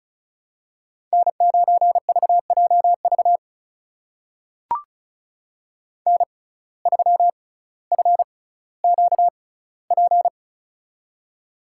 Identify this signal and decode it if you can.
Morse code, heard as a keyed tone, about 36 wpm, spaced as for 18 wpm, 700 Hz. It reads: N9VJV N3FQP
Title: USA Call Signs - 5 Characters - 6x Character Spacing - 35wpm